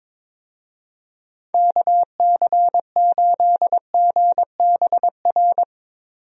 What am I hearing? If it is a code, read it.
Morse code, heard as a keyed tone, about 22 wpm, 700 Hz. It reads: KC8GBR
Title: USA Call Signs - 6 Characters - Repeated 3x - 22wpm